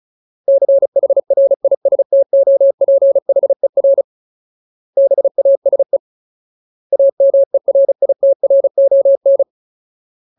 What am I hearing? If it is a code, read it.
Morse code, heard as a keyed tone, about 34 wpm, 550 Hz. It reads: CHRISTOPHER BASE AMERITRON
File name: Ultimate Word List / 35WPM Ultimate